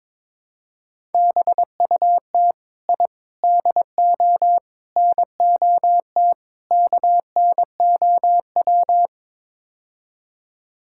Morse code, heard as a keyed tone, about 22 wpm, 700 Hz. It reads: BUT I DO NOT KNOW
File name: practice / 22wpm / Build A Phrase - 5-Word Phrases - 22wpm